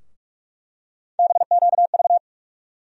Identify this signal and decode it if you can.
Morse code, heard as a keyed tone, about 42 wpm, 700 Hz. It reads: BQV
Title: USA Call Sign Postfixes - 3 Letters 45wpm